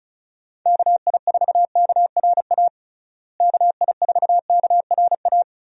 Morse code, heard as a keyed tone, about 36 wpm, 700 Hz. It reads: KI4KRA KI4KRA
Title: USA Call Signs - 6 Characters - Repeated 3x - 35wpm